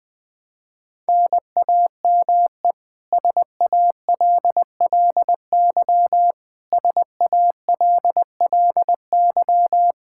Morse code, heard as a keyed tone, about 20 wpm, 700 Hz. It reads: NAME SALLY SALLY
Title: K1USN SST QSO Practice 20wpm